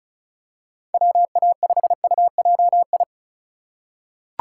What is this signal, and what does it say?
Morse code, heard as a keyed tone, about 36 wpm, 690 Hz. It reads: WA5UJI
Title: USA Call Signs - 6 Characters - 3x Character Spacing - 35wpm